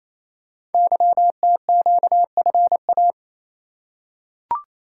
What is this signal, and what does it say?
Morse code, heard as a keyed tone, about 27 wpm, 700 Hz. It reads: YTQFA
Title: Code Groups - 5 Letters - 28wpm